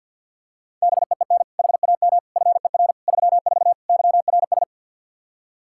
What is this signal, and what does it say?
Morse code, heard as a keyed tone, about 46 wpm, 700 Hz. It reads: BEEN HAM FER 34 YRS
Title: Practice QSO - Exchange 3 - More Info - 50wpm